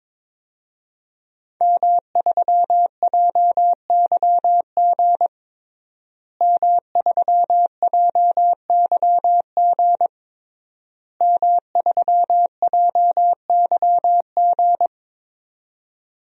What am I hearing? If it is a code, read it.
Morse code, heard as a keyed tone, about 22 wpm, 700 Hz. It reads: M3JYG M3JYG M3JYG
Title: DXCC-100 Call Signs - 5 Characters - Repeated 3x - 22wpm